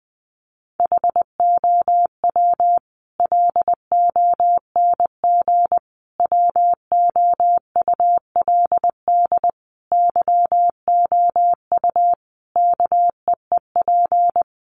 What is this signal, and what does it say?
Morse code, heard as a keyed tone, about 20 wpm, 700 Hz. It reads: HOW LONG WOULD YOU KEEP
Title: Sentences from Top 500 Words 20wpm